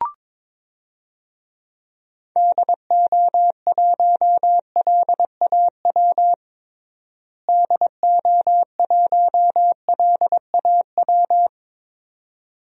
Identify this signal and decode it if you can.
Morse code, heard as a keyed tone, about 22 wpm, 700 Hz. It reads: DO1LAW DO1LAW
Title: DXCC-100 Call Signs - 6 Characters - Repeated 2x - 22wpm